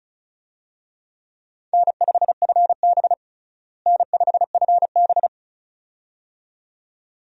Morse code, heard as a keyed tone, about 36 wpm, 700 Hz. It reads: N5FB N5FB
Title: USA Call Signs - 4 Characters - Repeated 2x - 35wpm